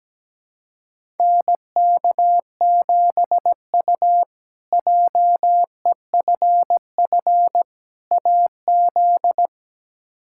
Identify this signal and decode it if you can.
Morse code, heard as a keyed tone, about 17 wpm, 700 Hz. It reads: NK7U JEFF AZ
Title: K1USN SST QSO Practice - 17wpm